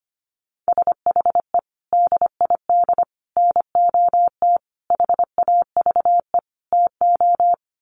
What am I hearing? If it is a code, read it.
Morse code, heard as a keyed tone, about 25 wpm, 700 Hz. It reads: SHE DID NOT HAVE TO
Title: Sets of 5 Words Encore2 25wpm